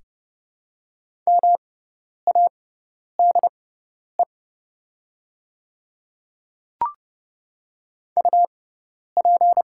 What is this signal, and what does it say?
Morse code, heard as a keyed tone, about 30 wpm, spaced as for 10 wpm, 690 Hz. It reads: MADE UP
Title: 6x Character Spacing - Top 500 Words - 30wpm